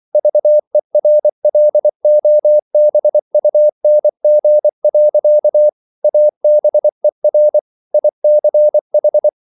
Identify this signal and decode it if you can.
Morse code, heard as a keyed tone, about 24 wpm, 600 Hz. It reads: VERLOBUNG. ABER ICH
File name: Liebespaar2 24wpm